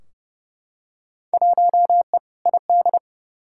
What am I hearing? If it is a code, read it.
Morse code, heard as a keyed tone, about 30 wpm, 700 Hz. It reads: 1E ID
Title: Field Day Exchanges - 30wpm